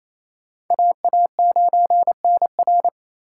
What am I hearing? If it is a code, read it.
Morse code, heard as a keyed tone, about 28 wpm, 700 Hz. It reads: AA9NR
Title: USA Call Signs - 5 Characters - 2x Character Spacing - 28wpm